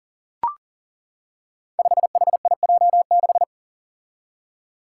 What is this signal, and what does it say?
Morse code, heard as a keyed tone, about 40 wpm, 700 Hz. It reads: 5HIJ6
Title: Five Challenging Letter-Number Combinations 40wpm